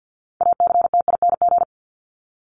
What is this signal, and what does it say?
Morse code, heard as a keyed tone, about 48 wpm, 700 Hz. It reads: ACTING
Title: ING Suffix - Top 100 Words with Base Words 50wpm